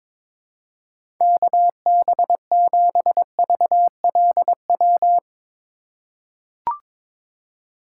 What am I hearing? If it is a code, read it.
Morse code, heard as a keyed tone, about 22 wpm, 700 Hz. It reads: KB7VLW
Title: USA Call Signs - 6 Characters - 3x Character Spacing - 22wpm